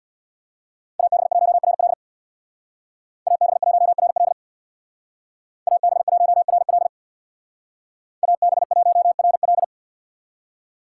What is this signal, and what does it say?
Morse code, heard as a keyed tone, about 65 wpm, 700 Hz. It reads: AB1RL AB1RL AB1RL AB1RL